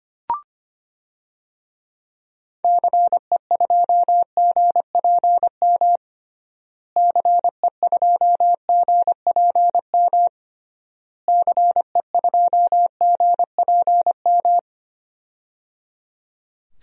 Morse code, heard as a keyed tone, about 25 wpm, 700 Hz. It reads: CE2GPM CE2GPM CE2GPM
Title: DXCC-100 Call Signs - 6 Characters - Repeated 3x - 25wpm